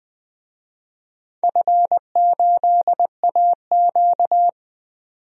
Morse code, heard as a keyed tone, about 20 wpm, 700 Hz. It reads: F8AQ
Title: DXCC-100 Call Signs - 4 Characters - 20wpm